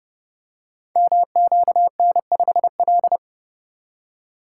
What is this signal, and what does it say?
Morse code, heard as a keyed tone, about 28 wpm, 700 Hz. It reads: MQN5L
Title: Code Groups - 5 Letter:Numbers - 30wpm